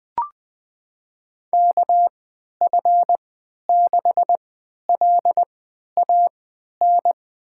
Morse code, heard as a keyed tone, about 20 wpm, spaced as for 11 wpm, 700 Hz. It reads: KF6LAN
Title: USA Call Signs - 6 Characters - 3x Character Spacing - 20wpm